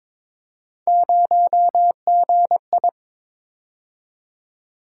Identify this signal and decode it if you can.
Morse code, heard as a keyed tone, about 22 wpm, 700 Hz. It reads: 0GI
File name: practice / 22wpm / Three Letter-Number Combinations - 22wpm